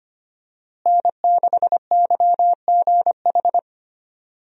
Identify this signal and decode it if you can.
Morse code, heard as a keyed tone, about 25 wpm, 700 Hz. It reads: N6YGH
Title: USA Call Signs - 5 Characters - Repeated 3x - 25wpm